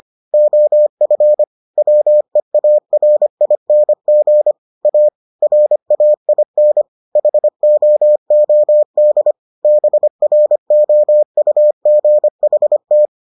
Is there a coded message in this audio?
Yes: Morse code from 0.3 to 13.1 s, heard as a keyed tone, about 25 wpm, 600 Hz. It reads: OF WEARING A RAIN HOOD BROUGHT